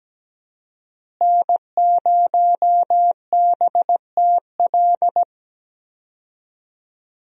Morse code, heard as a keyed tone, about 17 wpm, 700 Hz. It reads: N0BTL
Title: USA Call Signs - 5 Characters - Repeated 3x - 17wpm